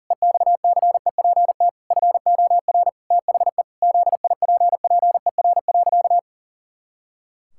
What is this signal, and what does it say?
Morse code, heard as a keyed tone, about 40 wpm, 700 Hz. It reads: EXCEPT FOR THE ZIPPER.
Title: Pangram Challenge v2 - 40wpm